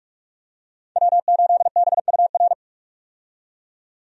Morse code, heard as a keyed tone, about 46 wpm, 700 Hz. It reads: W8BUR
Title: USA Call Signs - 5 Characters - 45wpm